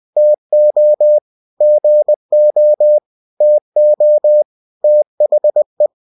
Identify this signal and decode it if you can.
Morse code, heard as a keyed tone, about 20 wpm, 600 Hz. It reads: TO GO TO THE
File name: cw-books / winnie-the-pooh / winnie-20wpm-page-0049